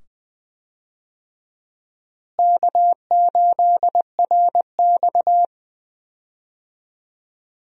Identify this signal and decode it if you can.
Morse code, heard as a keyed tone, about 20 wpm, 700 Hz. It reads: K8RX
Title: USA Call Signs - 4 Characters - Repeated 2x - 20wpm